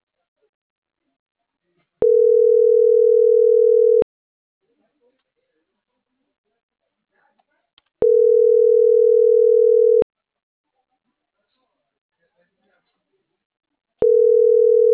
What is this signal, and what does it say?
Ringback tone (North America), 440 + 480 Hz.